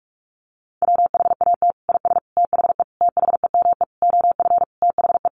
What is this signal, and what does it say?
Morse code, heard as a keyed tone, about 44 wpm, 700 Hz. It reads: WHAT IS THE THEME OF THE
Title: Sentences from Top 1100 Words 45wpm